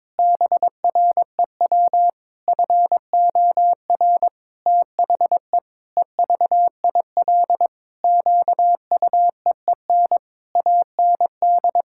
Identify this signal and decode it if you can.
Morse code, heard as a keyed tone, about 22 wpm, 700 Hz. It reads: BREW FOR THE EVIL QUEEN AND
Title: Pangram Challenge v2 - 22wpm